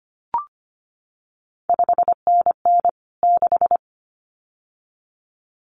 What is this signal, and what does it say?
Morse code, heard as a keyed tone, about 25 wpm, 700 Hz. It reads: 5NN 6
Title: CQ WW DX Contest Exchange Encore - 25wpm